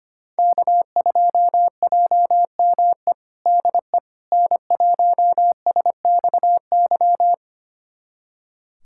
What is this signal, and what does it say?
Morse code, heard as a keyed tone, about 25 wpm, 700 Hz. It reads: K2JME DE N1SXY
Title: USA Callsign DE Callsign - 5 Characters - 25wpm